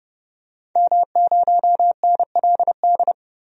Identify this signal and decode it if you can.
Morse code, heard as a keyed tone, about 30 wpm, 700 Hz. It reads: M0NLD
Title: DXCC-100 Call Signs - 5 Characters - Repeated 2x - 30wpm